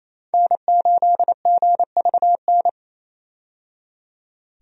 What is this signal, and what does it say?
Morse code, heard as a keyed tone, about 28 wpm, 700 Hz. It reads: N8GVN
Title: USA Call Signs - 5 Characters - 28wpm